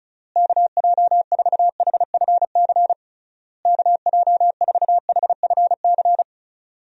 Morse code, heard as a keyed tone, about 36 wpm, 700 Hz. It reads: KJ4HFC KJ4HFC
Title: USA Call Signs - 6 Characters - Repeated 3x - 35wpm